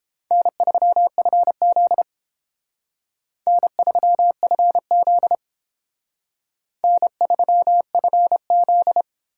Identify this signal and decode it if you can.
Morse code, heard as a keyed tone, about 30 wpm, 700 Hz. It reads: N3FZ N3FZ N3FZ